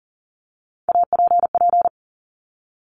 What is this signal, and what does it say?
Morse code, heard as a keyed tone, about 38 wpm, 700 Hz. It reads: APP
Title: Top Three Letter Combinations in Words 40wpm